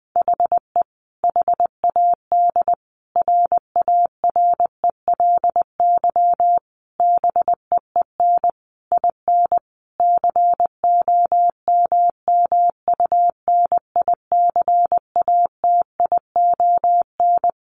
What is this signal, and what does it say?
Morse code, heard as a keyed tone, about 20 wpm, 700 Hz. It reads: HE HAD RARELY BEEN IN COMMUNICATION
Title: Sentences from Top 1400 Words 20wpm